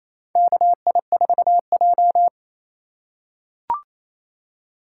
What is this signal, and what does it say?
Morse code, heard as a keyed tone, about 28 wpm, 700 Hz. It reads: KI4J
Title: USA Call Signs - 2x Character Spacing - 28wpm